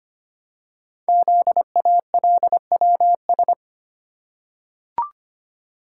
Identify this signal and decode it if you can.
Morse code, heard as a keyed tone, about 25 wpm, 700 Hz. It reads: ZALWS
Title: Code Groups - 5 Letters - 25wpm